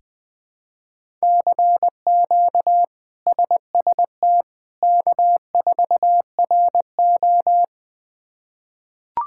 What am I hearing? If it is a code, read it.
Morse code, heard as a keyed tone, about 20 wpm, 700 Hz. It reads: CQ SST K4RO
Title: K1USN SST QSO Practice - 20wpm